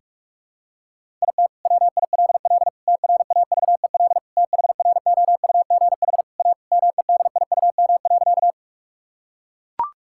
Morse code, heard as a keyed tone, about 42 wpm, 700 Hz. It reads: IT WILL TRAVEL THROUGH A MEDIUM.
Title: Sentences from Top 1000 Words - Review Part 6 - 45wpm